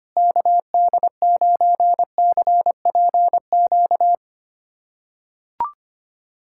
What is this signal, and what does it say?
Morse code, heard as a keyed tone, about 25 wpm, 700 Hz. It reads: KD9CPQ
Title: USA Callsign DE Callsign 25wpm